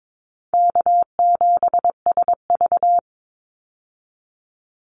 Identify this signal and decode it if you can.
Morse code, heard as a keyed tone, about 22 wpm, 700 Hz. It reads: K7SV
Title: Call Signs - 22wpm